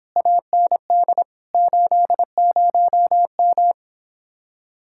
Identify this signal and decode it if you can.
Morse code, heard as a keyed tone, about 26 wpm, 700 Hz. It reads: AND 80M